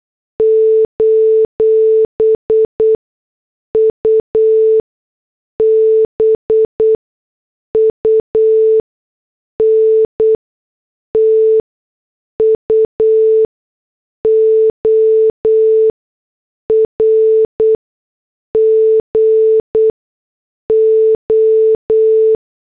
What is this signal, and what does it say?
Morse code, heard as a keyed tone, about 8 wpm, spaced as for 6 wpm, 440 Hz. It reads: : UBUNTU OR GO
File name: morse / headlines